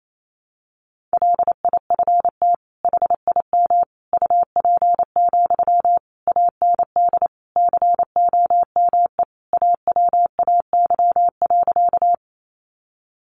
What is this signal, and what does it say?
Morse code, heard as a keyed tone, about 28 wpm, 700 Hz. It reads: LIFT HIM UP, AND COME AWAY.
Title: Sentences from Top 1000 Words 28wpm